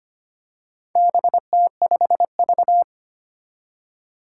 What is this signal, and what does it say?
Morse code, heard as a keyed tone, about 25 wpm, 700 Hz. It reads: BT5V
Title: Four Challenging Letter-Number Combinations 25wpm